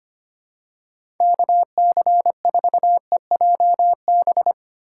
Morse code, heard as a keyed tone, about 25 wpm, 700 Hz. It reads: KC4EJB
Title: USA Call Signs - 6 Characters - 2x Character Spacing - 25wpm